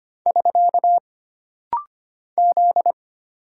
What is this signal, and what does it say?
Morse code, heard as a keyed tone, about 25 wpm, 700 Hz. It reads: <SK> Z
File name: Single Character (Letter:Number:Punctuation:Prosign) - Rapid-Fire - 25wpm